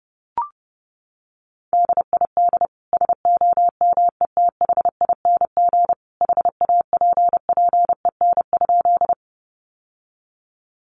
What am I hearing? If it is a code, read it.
Morse code, heard as a keyed tone, about 30 wpm, 700 Hz. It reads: DID SOMETHING HAPPEN?
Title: Sentences from Top 600 Words 30wpm